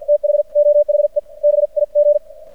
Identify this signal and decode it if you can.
Morse code, heard as a keyed tone, about 55 wpm, spaced as for 45 wpm, 590 Hz. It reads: THOHE BIG
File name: archived / CRSnet